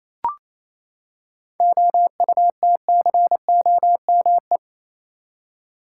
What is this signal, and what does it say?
Morse code, heard as a keyed tone, about 28 wpm, 700 Hz. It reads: OUTCOME
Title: Seven Letter Words - 28wpm